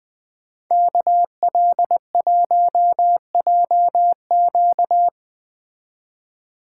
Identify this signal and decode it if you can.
Morse code, heard as a keyed tone, about 20 wpm, 700 Hz. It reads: KL1JQ